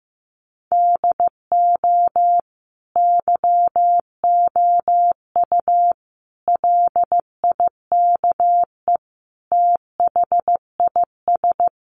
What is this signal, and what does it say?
Morse code, heard as a keyed tone, about 15 wpm, 700 Hz. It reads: DO YOU LIKE THIS
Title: Sentences from Top 1200 Words 15wpm